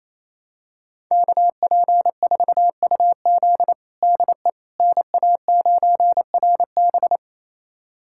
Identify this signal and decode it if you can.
Morse code, heard as a keyed tone, about 27 wpm, 700 Hz. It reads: KP4UZ DE NA9RB
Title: USA Callsign DE Callsign - 5 Characters - 28wpm